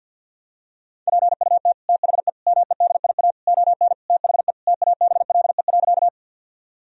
Morse code, heard as a keyed tone, about 50 wpm, 700 Hz. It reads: PUT THE MEDIA ON THE TABLE.
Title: Sentences from Top 1000 Words - Review Part 4 - 50wpm